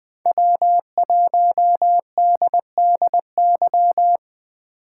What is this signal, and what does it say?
Morse code, heard as a keyed tone, about 20 wpm, 700 Hz. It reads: W1DDY
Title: Speed Racing - USA 1x3 Call Signs 17wpm